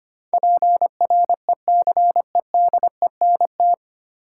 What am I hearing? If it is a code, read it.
Morse code, heard as a keyed tone, about 25 wpm, 700 Hz. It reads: PRECEDENT
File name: practice / 25wpm / 9 Letter Words - First 3 Letters Given - 25wpm